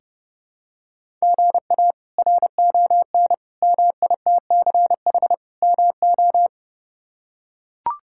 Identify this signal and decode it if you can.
Morse code, heard as a keyed tone, about 30 wpm, 700 Hz. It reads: GA RON MITCH MO
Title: K1USN SST QSO Practice - 30wpm